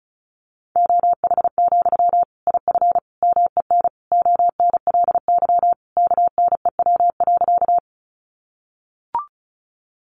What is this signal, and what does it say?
Morse code, heard as a keyed tone, about 34 wpm, 700 Hz. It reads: OH, IF MEN ONLY KNEW.
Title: Sentences from Top 500 Words 35wpm